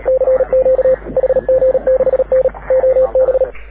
Morse code, heard as a keyed tone, about 37 wpm, 530 Hz. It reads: CQ HG6N OB
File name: AUDIO2019 / HG6N CW SSB MIX